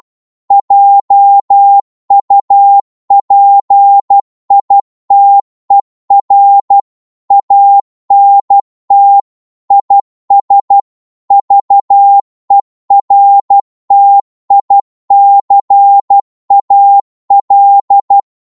Morse code, heard as a keyed tone, about 12 wpm, 800 Hz. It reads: JUPITER ANT IS VERTICAL
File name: Morse / qso-12wpm